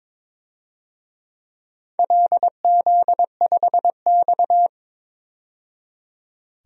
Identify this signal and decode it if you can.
Morse code, heard as a keyed tone, about 22 wpm, 700 Hz. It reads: LZ5X
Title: DXCC-100 Call Signs - 4 Characters - 22wpm